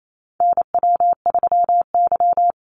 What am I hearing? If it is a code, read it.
Morse code, heard as a keyed tone, about 28 wpm, 700 Hz. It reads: NW3Y
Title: Call Signs - 28wpm